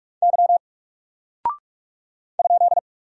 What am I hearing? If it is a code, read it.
Morse code, heard as a keyed tone, about 46 wpm, 700 Hz. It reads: Y ?
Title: Single Character (Letter:Number:Punctuation:Prosign) 45wpm